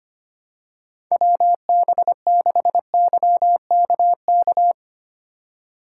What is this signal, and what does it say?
Morse code, heard as a keyed tone, about 25 wpm, 700 Hz. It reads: WB6YKK
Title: USA Call Signs - 6 Characters - 3x Character Spacing - 25wpm